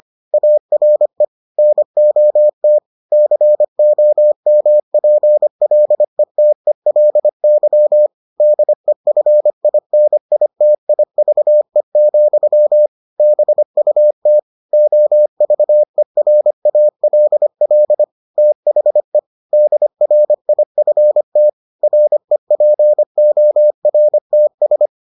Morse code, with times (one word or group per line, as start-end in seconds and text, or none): 0.3-1.2 ARE
1.6-2.8 NOT
3.1-8.1 COMPLETELY
8.4-12.9 DEFINITIVE,
13.2-14.4 BUT
14.7-18.0 OVERALL
18.4-19.2 THE
19.5-21.5 DRIFT
21.8-24.9 REPORTS